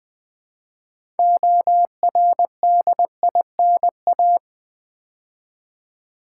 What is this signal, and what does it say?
Morse code, heard as a keyed tone, about 20 wpm, 700 Hz. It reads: ORDINA
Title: Build A Word - 3 Letters At A Time - 8-Letter Words - 20wpm